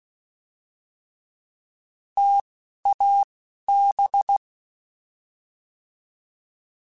Morse code, heard as a keyed tone, about 16 wpm, spaced as for 11 wpm, 800 Hz. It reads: TAB
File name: T Prefixes / T mp3s